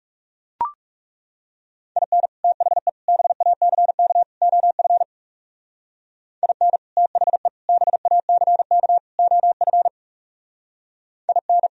Morse code, heard as a keyed tone, about 37 wpm, 700 Hz. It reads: IN THE BACK OF IN THE BACK OF IN
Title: Speed-Racing - Sets of 4 Words 30wpm